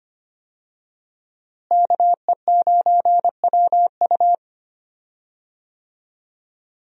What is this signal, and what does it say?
Morse code, heard as a keyed tone, about 25 wpm, 700 Hz. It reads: KE9WU
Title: USA Call Signs - 5 Characters - Repeated 2x - 25wpm